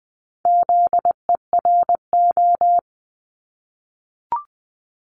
Morse code, heard as a keyed tone, about 20 wpm, 700 Hz. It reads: ZERO
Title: Multi-letter Phonemes 20wpm